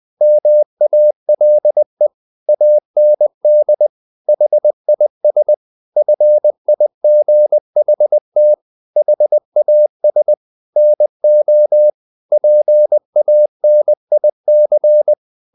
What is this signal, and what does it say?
Morse code, heard as a keyed tone, about 20 wpm, 600 Hz. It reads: MALE AND HIS FIGHT HAS NO PANIC
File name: the old man and the sea / the old man and the sea part 1 20wpm